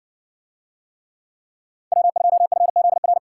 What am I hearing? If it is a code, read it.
Morse code, heard as a keyed tone, about 60 wpm, 700 Hz. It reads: W2FZR